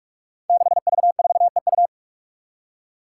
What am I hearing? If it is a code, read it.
Morse code, heard as a keyed tone, about 44 wpm, 700 Hz. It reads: 6V4EU
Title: Five Challenging Letter-Number Combinations 45wpm